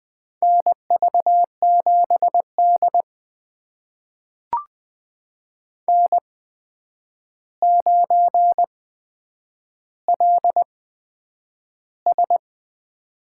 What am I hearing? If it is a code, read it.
Morse code, heard as a keyed tone, about 20 wpm, spaced as for 9 wpm, 700 Hz. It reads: NV7D N9LS
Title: USA Call Signs - 4 Characters - 8x Character Spacing - 20wpm